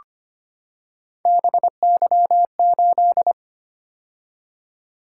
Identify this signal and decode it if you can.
Morse code, heard as a keyed tone, about 25 wpm, 700 Hz. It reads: BY8
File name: Three Letter-Number Combinations - 25wpm